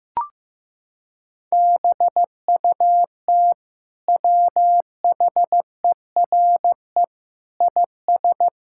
Morse code, heard as a keyed tone, about 15 wpm, 700 Hz. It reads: BUT WHERE IS
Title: Sentences from Top 1000 Words - Review Part 5 - 15wpm